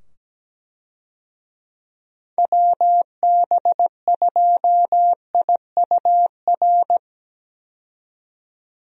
Morse code, heard as a keyed tone, about 17 wpm, 700 Hz. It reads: WB2IUR
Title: USA Call Signs - 6 Characters - 17wpm